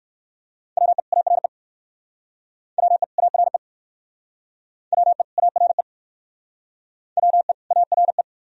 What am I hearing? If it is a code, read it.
Morse code, heard as a keyed tone, about 62 wpm, 700 Hz. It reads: WE ARE WE ARE WE ARE WE ARE